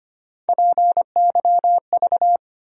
Morse code, heard as a keyed tone, about 25 wpm, 700 Hz. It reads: PYV
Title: USA Call Sign Postfixes - 3 Letters 25wpm